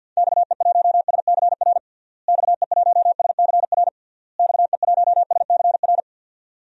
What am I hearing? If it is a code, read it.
Morse code, heard as a keyed tone, about 51 wpm, 700 Hz. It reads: XE1SCR XE1SCR XE1SCR
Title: DXCC-100 Call Signs - 6 Characters - Repeated 3x - 50wpm